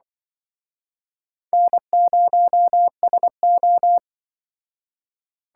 Morse code, heard as a keyed tone, about 25 wpm, 690 Hz. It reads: N0SO
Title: Speed Racing - USA 1x2 Call Signs 20wpm